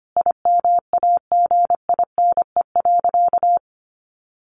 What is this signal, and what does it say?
Morse code, heard as a keyed tone, about 25 wpm, 700 Hz. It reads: IMAGINE.
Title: Sentences from Top 1400 Words 25wpm